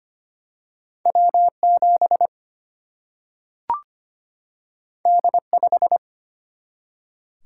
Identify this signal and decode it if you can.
Morse code, heard as a keyed tone, about 25 wpm, 700 Hz. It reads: W 7 D 5
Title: Single-Letter Single-Number 25wpm